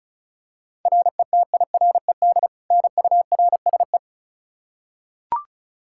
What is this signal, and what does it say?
Morse code, heard as a keyed tone, about 36 wpm, 700 Hz. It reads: RETIRED NURSE
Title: Practice QSO - Exchange 3 - More Info - 35wpm